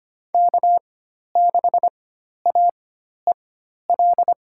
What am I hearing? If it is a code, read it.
Morse code, heard as a keyed tone, about 25 wpm, spaced as for 12 wpm, 700 Hz. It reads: K6AEL
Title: USA Call Signs - 5 Characters - 4x Character Spacing - 25wpm